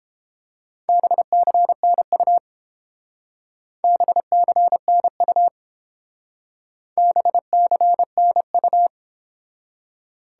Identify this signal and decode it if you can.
Morse code, heard as a keyed tone, about 29 wpm, 700 Hz. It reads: BCNU BCNU BCNU